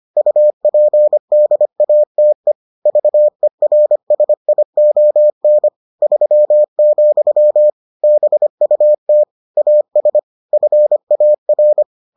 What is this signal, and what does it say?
Morse code, heard as a keyed tone, about 25 wpm, 600 Hz. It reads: UPDATE VERSION 3, BUT AS FAR